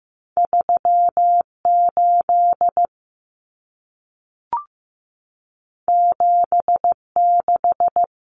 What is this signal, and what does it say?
Morse code, heard as a keyed tone, about 15 wpm, 700 Hz. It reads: 38 76
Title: Common Cut Numbers - 2-Digit 15wpm